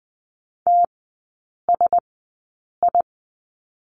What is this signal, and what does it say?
Morse code, heard as a keyed tone, about 20 wpm, spaced as for 8 wpm, 700 Hz. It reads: TSI
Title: 035 Lesson 7e - Letter s - Three Characters 20wpm